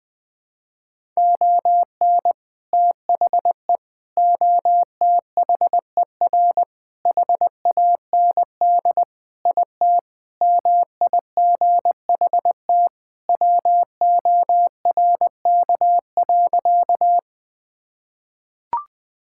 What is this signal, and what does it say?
Morse code, heard as a keyed tone, about 20 wpm, 700 Hz. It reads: ON THE OTHER HAND IT MIGHT WORK.
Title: Sentences with 3-Word Phrases - Spoken Beginning - 20wpm